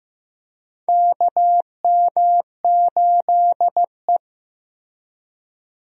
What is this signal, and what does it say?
Morse code, heard as a keyed tone, about 15 wpm, 700 Hz. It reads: KM8E
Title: USA Call Signs - 4 Characters - Repeated 2x - 15wpm